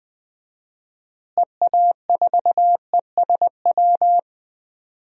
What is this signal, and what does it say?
Morse code, heard as a keyed tone, about 20 wpm, 700 Hz. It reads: EA4ESW
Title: DXCC-100 Call Signs - 6 Characters - Repeated 2x - 20wpm